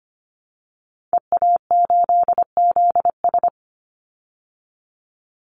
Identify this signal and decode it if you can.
Morse code, heard as a keyed tone, about 25 wpm, 700 Hz. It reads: EA8ZS
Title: Call Signs Repeated 3x - 25wpm